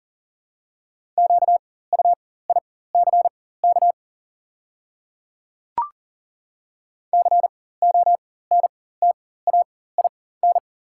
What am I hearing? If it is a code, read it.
Morse code, heard as a keyed tone, about 40 wpm, spaced as for 19 wpm, 700 Hz. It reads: QUICK CONTAIN
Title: 4x Character Spacing - Top 500 Words - 40wpm